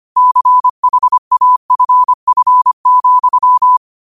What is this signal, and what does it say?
Morse code, heard as a keyed tone, about 25 wpm, 1000 Hz. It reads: CHAFF,
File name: qotdfiles / 25wpm